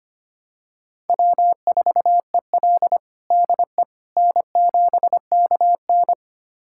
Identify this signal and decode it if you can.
Morse code, heard as a keyed tone, about 25 wpm, 700 Hz. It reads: W4EL DE N7KN
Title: USA Callsign DE Callsign - 4 Characters - 25wpm